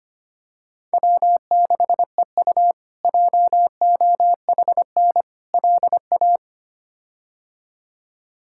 Morse code, heard as a keyed tone, about 25 wpm, 700 Hz. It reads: W6EU JOHN LA
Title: K1USN SST QSO Practice - 25wpm